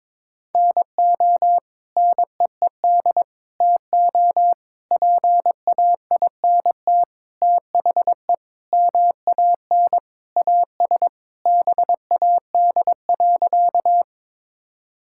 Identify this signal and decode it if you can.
Morse code, heard as a keyed tone, about 21 wpm, 700 Hz. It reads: NO NEED TO PAINT THE MAN AS BAD.
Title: Sentences from Top 1000 Words - Review Part 6 - 22wpm